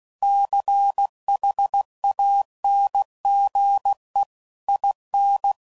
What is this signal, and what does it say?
Morse code, heard as a keyed tone, about 16 wpm, 800 Hz. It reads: CHANGE IN
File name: CW Code Groups @15WPM / Technical Subjects / XC